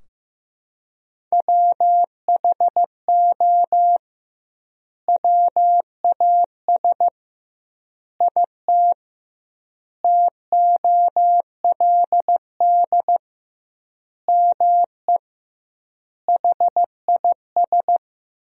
Morse code, heard as a keyed tone, about 15 wpm, 700 Hz. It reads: WHO WAS IT TOLD ME HIS
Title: 2x Word-Spacing - Sentences from Top 400 Words - 15wpm